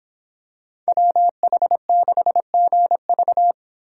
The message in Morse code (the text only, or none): WH6GV